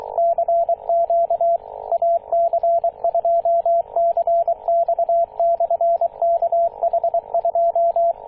Morse code, heard as a keyed tone, about 23 wpm, 680 Hz, with radio noise. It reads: CQ AC2CX/KH2
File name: posts / 2016-june-26-0520z-ac2cx-kh2-21-027mhz-cw